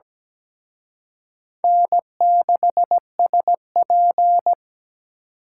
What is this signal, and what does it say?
Morse code, heard as a keyed tone, about 20 wpm, 700 Hz. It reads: N6SP
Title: Speed Racing - USA Call Signs 17wpm